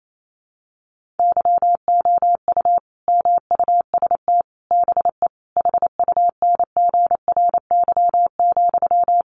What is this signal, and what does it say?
Morse code, heard as a keyed tone, about 28 wpm, 700 Hz. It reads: YOU MUST BE HUNGRY,
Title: Sentences from Top 1400 Words 28wpm